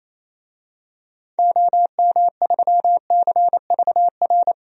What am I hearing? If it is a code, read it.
Morse code, heard as a keyed tone, about 28 wpm, 700 Hz. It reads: OM3CVR
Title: DXCC-100 Call Signs - 6 Characters - 28wpm